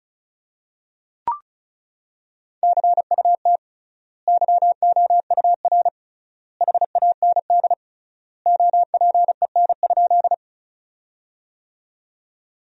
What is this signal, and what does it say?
Morse code, heard as a keyed tone, about 36 wpm, 700 Hz. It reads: CUT YOUR HAND OPEN?
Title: 3x Word-Spacing - Sentences from Top 300 Words - 35wpm